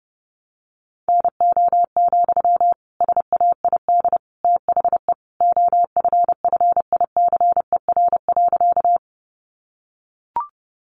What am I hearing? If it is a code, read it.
Morse code, heard as a keyed tone, about 30 wpm, 700 Hz. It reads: NO, SAID THE OFFICER.
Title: Sentences from Top 1200 Words 30wpm